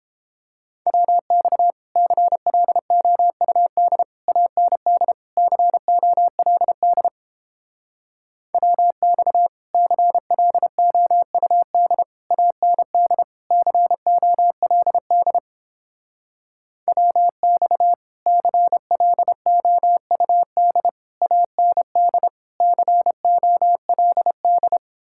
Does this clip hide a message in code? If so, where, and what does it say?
Yes: Morse code from 0.9 to 24.8 s, heard as a keyed tone, about 31 wpm, 700 Hz. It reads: WX CLOUD AND COLD WX CLOUD AND COLD WX CLOUD AND COLD